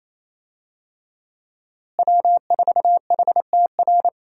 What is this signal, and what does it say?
Morse code, heard as a keyed tone, about 28 wpm, 700 Hz. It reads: W4HTR
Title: USA Call Signs - 5 Characters - Repeated 3x - 28wpm